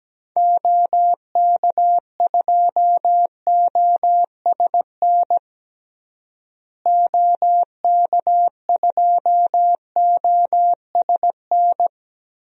Morse code, heard as a keyed tone, about 17 wpm, 700 Hz. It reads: OK2OSN OK2OSN
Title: DXCC-100 Call Signs - 6 Characters - Repeated 3x - 17wpm